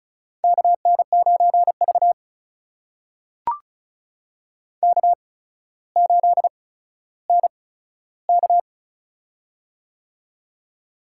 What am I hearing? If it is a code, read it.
Morse code, heard as a keyed tone, about 36 wpm, 700 Hz. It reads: K N 9 V K 8 N K
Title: USA Call Signs - 8x Character Spacing - 35wpm